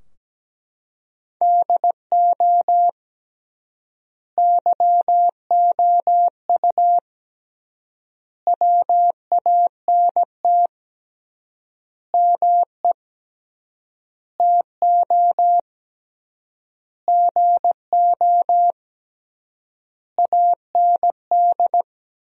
Morse code, heard as a keyed tone, about 17 wpm, 700 Hz. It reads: DO YOU WANT ME TO GO AND
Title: 3x Word-Spacing - Sentences from Top 200 Words - 17wpm